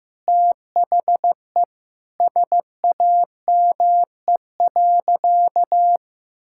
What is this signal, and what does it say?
Morse code, heard as a keyed tone, about 15 wpm, 700 Hz. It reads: THE SAME.
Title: Sentences from Top 1000 Words - Review Part 5 - 15wpm